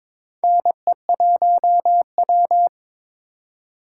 Morse code, heard as a keyed tone, about 22 wpm, 700 Hz. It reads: NE1W
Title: USA Call Signs - 4 Characters - 2x Character Spacing - 22wpm